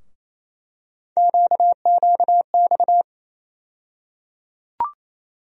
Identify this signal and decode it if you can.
Morse code, heard as a keyed tone, about 28 wpm, 700 Hz. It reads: QQX
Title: USA Call Sign Postfixes - 3 Letters 28wpm